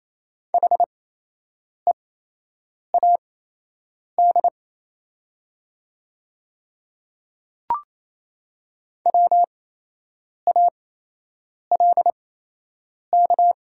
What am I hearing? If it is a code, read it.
Morse code, heard as a keyed tone, about 28 wpm, spaced as for 8 wpm, 700 Hz. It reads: HEAD WALK
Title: 8x Character Spacing - Top 300 Words - 28wpm